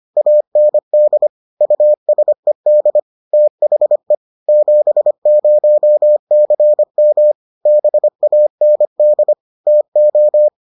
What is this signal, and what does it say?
Morse code, heard as a keyed tone, about 25 wpm, 600 Hz. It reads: AND USED THE 70CM BAND TO